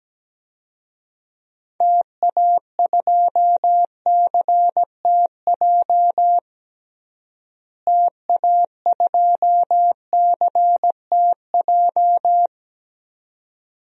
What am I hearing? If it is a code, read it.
Morse code, heard as a keyed tone, about 17 wpm, 700 Hz. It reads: TA2CTJ TA2CTJ
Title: DXCC-100 Call Signs - 6 Characters - Repeated 2x - 17wpm